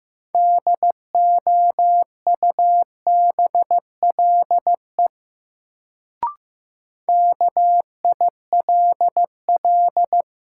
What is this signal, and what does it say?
Morse code, heard as a keyed tone, about 15 wpm, 700 Hz. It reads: DOUBLE KILL
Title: Sentences from Top 1400 Words - New Word Drill - 15wpm